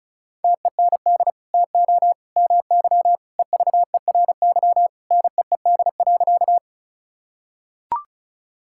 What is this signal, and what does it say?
Morse code, heard as a keyed tone, about 34 wpm, 700 Hz. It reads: TEND TO MY EVERY NEED.
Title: Sentences from Top 1000 Words - Review Part 5 - 35wpm